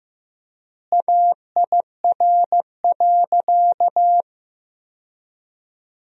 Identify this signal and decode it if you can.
Morse code, heard as a keyed tone, about 15 wpm, 700 Hz. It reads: AIR.
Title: 4x Word-Spacing - Sentences from Top 200 Words - 15wpm